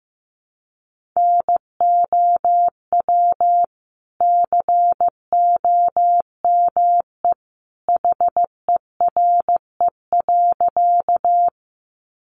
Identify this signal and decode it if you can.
Morse code, heard as a keyed tone, about 15 wpm, 700 Hz. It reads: NOW COME HERE.
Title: Sentences from Top 200 Words 15wpm